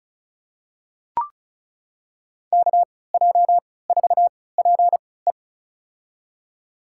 Morse code, heard as a keyed tone, about 36 wpm, spaced as for 20 wpm, 700 Hz. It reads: KJ4PE
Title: USA Call Signs - 3x Character Spacing - 35wpm